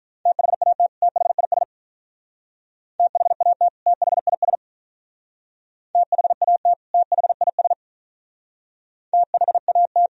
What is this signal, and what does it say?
Morse code, heard as a keyed tone, about 43 wpm, 700 Hz. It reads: THAT THIS THAT THIS THAT THIS THAT
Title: Speed-Racing - Sets of 2 Words 35wpm